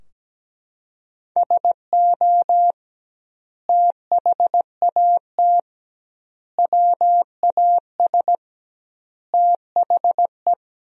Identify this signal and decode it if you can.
Morse code, heard as a keyed tone, about 17 wpm, 700 Hz. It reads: SO THAT WAS THE
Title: 2x Word-Spacing - Sentences from Top 100 Words - 17wpm